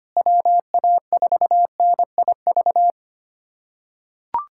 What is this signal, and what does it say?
Morse code, heard as a keyed tone, about 25 wpm, 700 Hz. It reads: WA4NIV
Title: K1USN SST QSO Practice - 25wpm